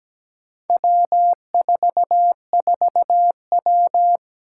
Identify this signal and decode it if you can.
Morse code, heard as a keyed tone, about 17 wpm, 700 Hz. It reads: W44W
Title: Four Challenging Letter-Number Combinations 17wpm